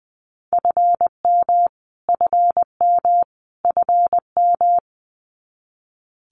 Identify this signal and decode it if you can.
Morse code, heard as a keyed tone, about 19 wpm, 700 Hz. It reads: FM FM FM
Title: Abbreviations Repeated 3x - 20wpm